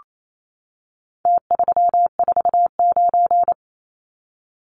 Morse code, heard as a keyed tone, about 28 wpm, 700 Hz. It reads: T349
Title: Numbers - Contest Style - 4-Digit 28wpm